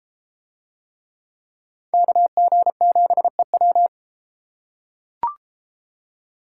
Morse code, heard as a keyed tone, about 32 wpm, 700 Hz. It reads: KG7EW